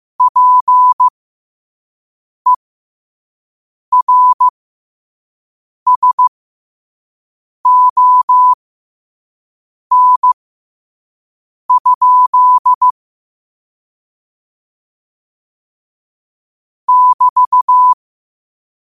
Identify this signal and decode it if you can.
Morse code, heard as a keyed tone, about 15 wpm, spaced as for 5 wpm, 1000 Hz. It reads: PERSON? <BT>
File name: qotdfiles / 5wpm